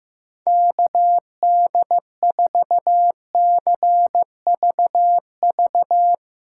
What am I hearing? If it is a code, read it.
Morse code, heard as a keyed tone, about 15 wpm, 700 Hz. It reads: KD4CVV
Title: USA Callsign DE Callsign - 6 Characters - 15wpm